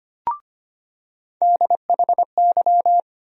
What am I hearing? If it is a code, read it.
Morse code, heard as a keyed tone, about 25 wpm, 700 Hz. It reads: DHY
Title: USA Call Sign Postfixes - 3 Letters 25wpm